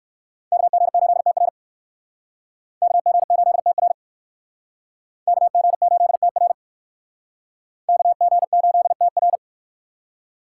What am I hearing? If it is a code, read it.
Morse code, heard as a keyed tone, about 58 wpm, 700 Hz. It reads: KG8TR KG8TR KG8TR KG8TR